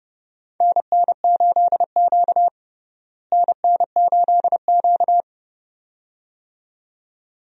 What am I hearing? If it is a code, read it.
Morse code, heard as a keyed tone, about 30 wpm, 700 Hz. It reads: NN8Q NN8Q
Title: USA Call Signs - 4 Characters - Repeated 2x - 30wpm